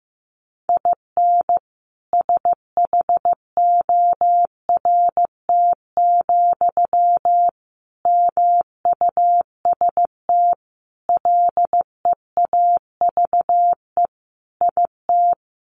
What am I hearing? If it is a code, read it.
Morse code, heard as a keyed tone, about 15 wpm, 700 Hz. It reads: IN SHORT, MUST LEAVE IT
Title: Sentences from Top 1000 Words 15wpm